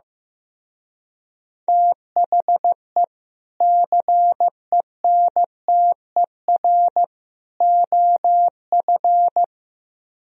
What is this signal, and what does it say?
Morse code, heard as a keyed tone, about 15 wpm, 700 Hz. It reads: THE CENTER OF
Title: Speed-Racing - Sets of 3 Words 15wpm